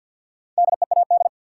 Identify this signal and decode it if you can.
Morse code, heard as a keyed tone, about 53 wpm, 700 Hz. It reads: DEAD
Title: Sentences from Top 1500 Words - New Word Drill - 50wpm